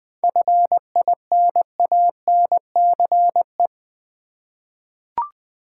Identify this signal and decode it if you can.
Morse code, heard as a keyed tone, about 20 wpm, 700 Hz. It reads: FINANCE
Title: Seven Letter Words - 20wpm